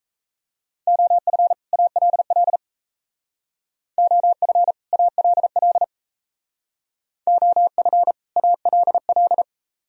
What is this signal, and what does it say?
Morse code, heard as a keyed tone, about 36 wpm, 700 Hz. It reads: OF ALL OF ALL OF ALL